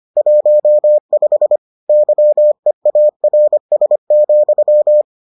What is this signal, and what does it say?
Morse code, heard as a keyed tone, about 25 wpm, 600 Hz. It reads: 15 YEARS,